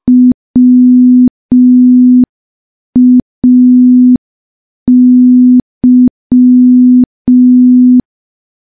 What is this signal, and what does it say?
Morse code, heard as a keyed tone, about 5 wpm, 250 Hz. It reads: WAY
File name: morse / morse-training-05WPM